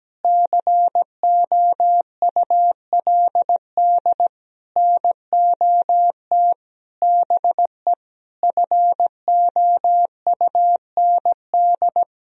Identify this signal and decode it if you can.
Morse code, heard as a keyed tone, about 17 wpm, 700 Hz. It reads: COULD NOT BE FOUND
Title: Sentences from Top 1000 Words - Review Part 1 - 17wpm